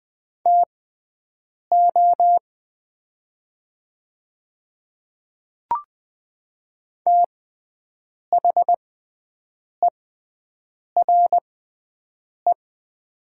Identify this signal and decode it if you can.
Morse code, heard as a keyed tone, about 20 wpm, spaced as for 7 wpm, 700 Hz. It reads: TO THERE
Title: 6x Character Spacing - Top 100 Words - 20wpm